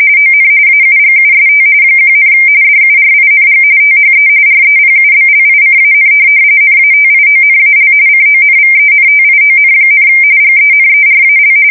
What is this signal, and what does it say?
RTTY, about 45.45 baud. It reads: ON AUGUST 1, DURING THE REPUBLICAN NATIONAL CONVENTION, GENERAL
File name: fsk170 / USS-New-Jersey